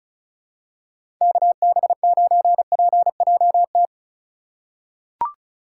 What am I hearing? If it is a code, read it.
Morse code, heard as a keyed tone, about 34 wpm, 700 Hz. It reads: KB9PJT
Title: USA Call Signs - 6 Characters - 2x Character Spacing - 35wpm